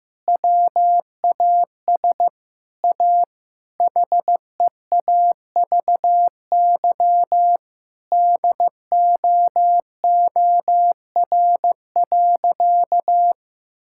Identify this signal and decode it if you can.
Morse code, heard as a keyed tone, about 15 wpm, 700 Hz. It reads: WAS A HEAVY DOOR.
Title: Sentences from Top 1600 Words 15wpm